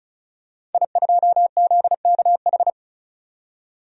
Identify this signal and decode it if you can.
Morse code, heard as a keyed tone, about 34 wpm, 700 Hz. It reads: I2ZKH
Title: DXCC-100 Call Signs - 5 Characters - 35wpm